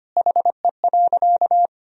Morse code, heard as a keyed tone, about 25 wpm, 700 Hz. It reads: HE.
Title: Sentences from Top 100 Words and Abbreviations 25wpm